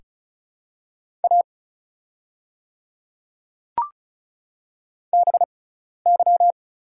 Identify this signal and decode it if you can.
Morse code, heard as a keyed tone, about 36 wpm, spaced as for 12 wpm, 700 Hz. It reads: A BY
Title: 6x Character Spacing - Top 100 Words - 35wpm